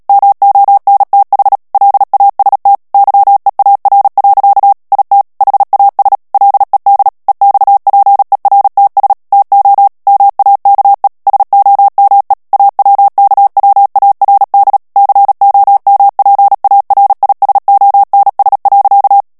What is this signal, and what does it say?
Morse code, heard as a keyed tone, about 37 wpm, 800 Hz. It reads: MONTH LAST YEAR. IT HAS LED EXPERTS TO MAKE SOME AWKWARD COMPARISONS.
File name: morse mp3 / ANGLAIS texte clair 2 vitesse 35 mots minute